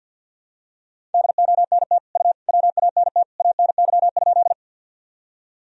Morse code, heard as a keyed tone, about 46 wpm, 700 Hz. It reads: DONT U WANT ANY?
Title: Sentences from Top 300 Words and Abbreviations 50wpm